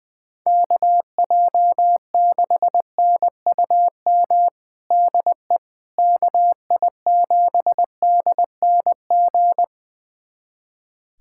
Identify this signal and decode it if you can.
Morse code, heard as a keyed tone, about 20 wpm, 700 Hz. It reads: KJ6NUM DE KI7DNG
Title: USA Callsign DE Callsign - 6 Characters - 20wpm